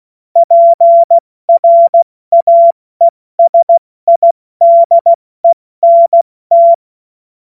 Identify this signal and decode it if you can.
Morse code, heard as a keyed tone, about 16 wpm, 680 Hz. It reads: PRAESIDENT
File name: Morsekurs / Witze / Witz 09. August 2022 70 F80